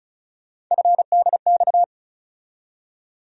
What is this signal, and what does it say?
Morse code, heard as a keyed tone, about 36 wpm, 700 Hz. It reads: FDX
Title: USA Call Sign Postfixes - 3 Letters 35wpm